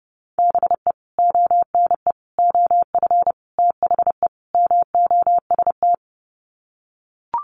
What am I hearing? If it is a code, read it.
Morse code, heard as a keyed tone, about 30 wpm, 700 Hz. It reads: BE ONE OF THE MOST
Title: Sets of 5 Words Encore 30wpm